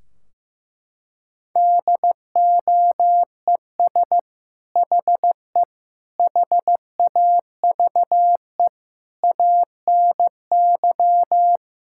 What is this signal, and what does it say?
Morse code, heard as a keyed tone, about 16 wpm, 700 Hz. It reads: DOES HE HAVE ANY
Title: Sentences from Top 1600 Words 15wpm